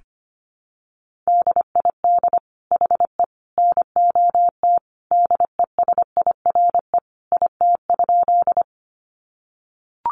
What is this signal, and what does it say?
Morse code, heard as a keyed tone, about 25 wpm, 700 Hz. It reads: DID HE NOT DESIRE IT?
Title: Sentences from Top 1100 Words 25wpm